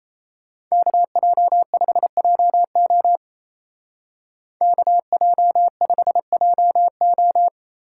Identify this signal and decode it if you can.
Morse code, heard as a keyed tone, about 34 wpm, 700 Hz. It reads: KJ5JO KJ5JO
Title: Speed Racing - USA 2x2 Call Signs 28wpm